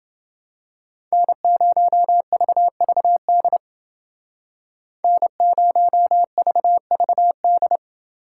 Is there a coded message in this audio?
Yes: Morse code from 1.1 to 7.8 s, heard as a keyed tone, about 28 wpm, 700 Hz. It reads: N0VVD N0VVD